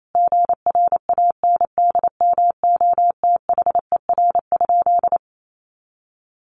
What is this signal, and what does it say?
Morse code, heard as a keyed tone, about 28 wpm, 700 Hz. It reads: GRANDMOTHER?
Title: Sentences from Top 1200 Words 28wpm